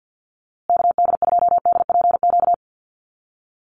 Morse code, heard as a keyed tone, about 50 wpm, 700 Hz. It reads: KD2DPQ
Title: USA Call Signs Repeated 2x 50wpm